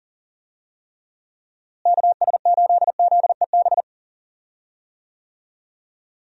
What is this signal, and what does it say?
Morse code, heard as a keyed tone, about 40 wpm, 700 Hz. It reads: KS8ZEB
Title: USA Call Signs - 6 Characters - Repeated 3x - 40wpm